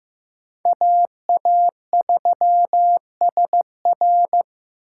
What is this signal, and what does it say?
Morse code, heard as a keyed tone, about 18 wpm, 700 Hz. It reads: AA3SR
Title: Speed Racing - USA 2x2 Call Signs 15wpm